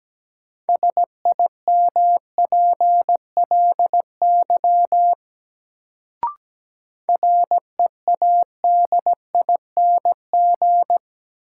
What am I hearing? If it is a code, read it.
Morse code, heard as a keyed tone, about 17 wpm, 700 Hz. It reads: SIMPLY READING
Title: Sentences from Top 600 Words - New Word Drill - 17wpm